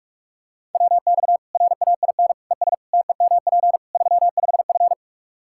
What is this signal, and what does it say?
Morse code, heard as a keyed tone, about 42 wpm, 700 Hz. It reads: WX RAIN ES TEMP 35F
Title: Practice QSO - Exchange 2 - Info - 45wpm